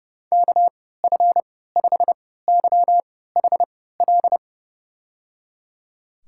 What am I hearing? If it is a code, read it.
Morse code, heard as a keyed tone, about 30 wpm, spaced as for 17 wpm, 700 Hz. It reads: KF5YHL
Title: USA Call Signs - 6 Characters - 3x Character Spacing - 30wpm